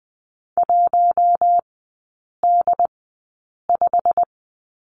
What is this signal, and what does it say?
Morse code, heard as a keyed tone, about 20 wpm, spaced as for 8 wpm, 700 Hz. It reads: 1D5
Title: 107 Lesson 18e - Letter m - Three Characters 20wpm